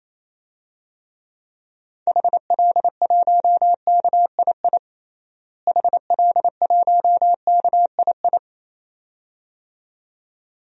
Morse code, heard as a keyed tone, about 28 wpm, 700 Hz. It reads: HL1KII HL1KII
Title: DXCC-100 Call Signs - 6 Characters - Repeated 2x - 28wpm